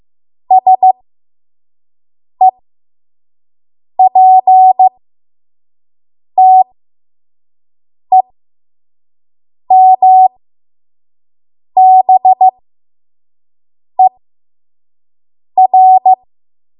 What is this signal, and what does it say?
Morse code, heard as a keyed tone, about 15 wpm, spaced as for 5 wpm, 750 Hz. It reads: SEPTEMBER
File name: Morse / 05WPM